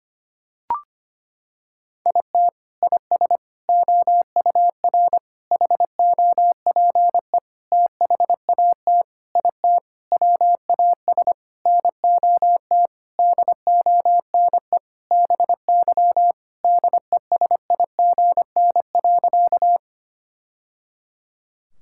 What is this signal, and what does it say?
Morse code, heard as a keyed tone, about 25 wpm, 700 Hz. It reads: IT IS OUR HOPE THAT IT WAS NOT DONE BY DESIGN.
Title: Sentences from Top 1000 Words - Review Part 5 - 25wpm